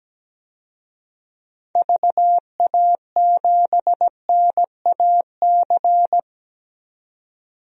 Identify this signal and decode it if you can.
Morse code, heard as a keyed tone, about 17 wpm, 700 Hz. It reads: VA7NAC
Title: DXCC-100 Call Signs - 6 Characters - 17wpm